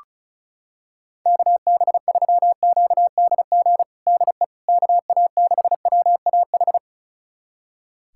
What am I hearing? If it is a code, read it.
Morse code, heard as a keyed tone, about 34 wpm, 690 Hz. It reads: KB3QDG DE KA6WAH
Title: USA Callsign DE Callsign - 6 Characters - 35wpm